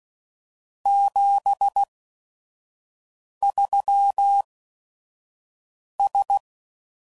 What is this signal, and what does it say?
Morse code, heard as a keyed tone, about 16 wpm, spaced as for 5 wpm, 800 Hz. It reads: 73S
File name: CW Code Groups @5WPM / QSOs / kc7haa